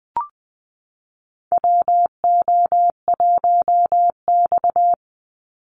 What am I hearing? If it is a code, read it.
Morse code, heard as a keyed tone, about 20 wpm, 700 Hz. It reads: WO1X
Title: Call Signs Repeated 2x - 20wpm